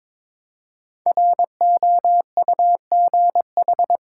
Morse code, heard as a keyed tone, about 23 wpm, 700 Hz. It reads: ROUGH
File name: practice / 22wpm / 5 Letter Words - First 3 Letters Given - 22wpm